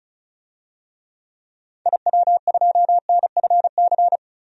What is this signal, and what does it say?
Morse code, heard as a keyed tone, about 36 wpm, 700 Hz. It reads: IW2NFC
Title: DXCC-100 Call Signs - 6 Characters - 35wpm